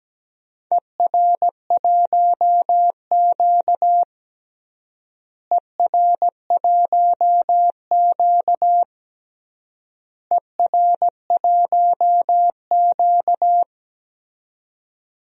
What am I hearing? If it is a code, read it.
Morse code, heard as a keyed tone, about 17 wpm, 700 Hz. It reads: ER1Q ER1Q ER1Q
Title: DXCC-100 Call Signs - 4 Characters - Repeated 3x - 17wpm